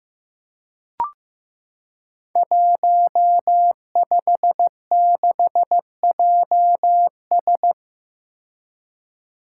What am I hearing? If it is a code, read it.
Morse code, heard as a keyed tone, about 15 wpm, 700 Hz. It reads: 156JS
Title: Five Challenging Letter-Number Combinations 15wpm